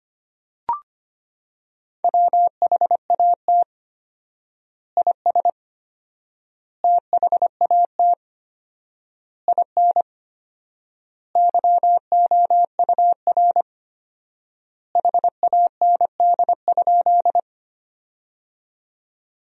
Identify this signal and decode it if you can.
Morse code, heard as a keyed tone, about 25 wpm, 700 Hz. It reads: WHAT IS THAT IN YOUR HAND?
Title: 4x Word-Spacing - Sentences from Top 200 Words - 25wpm